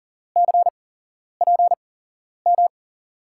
Morse code, heard as a keyed tone, about 39 wpm, spaced as for 11 wpm, 700 Hz. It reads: CPM
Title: USA Call Signs - 6 Characters - 8x Character Spacing - 40wpm